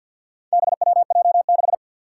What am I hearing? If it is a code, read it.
Morse code, heard as a keyed tone, about 51 wpm, 700 Hz. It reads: BWJ6
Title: Four Challenging Letter-Number Combinations 50wpm